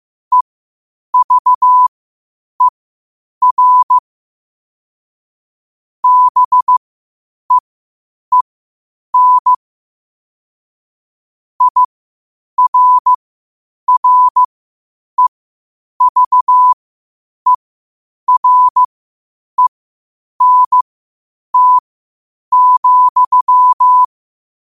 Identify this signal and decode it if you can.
Morse code, heard as a keyed tone, about 15 wpm, spaced as for 8 wpm, 1000 Hz. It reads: EVER BEEN IRREVERENT,
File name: qotdfiles / 8wpm